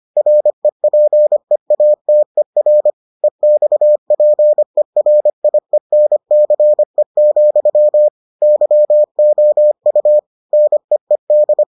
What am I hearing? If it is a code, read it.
Morse code, heard as a keyed tone, about 25 wpm, 600 Hz. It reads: REPEATER EXPERIENCE, YOU NEED